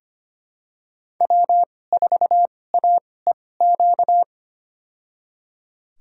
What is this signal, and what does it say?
Morse code, heard as a keyed tone, about 25 wpm, spaced as for 18 wpm, 700 Hz. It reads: W4AEQ
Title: USA Call Signs - 5 Characters - 2x Character Spacing - 25wpm